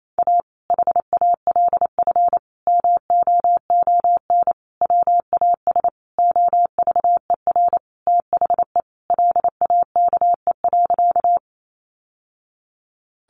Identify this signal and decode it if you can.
Morse code, heard as a keyed tone, about 28 wpm, 700 Hz. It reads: A HALF MOON WAS OVER THE LAKE.
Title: Sentences from Top 1100 Words 28wpm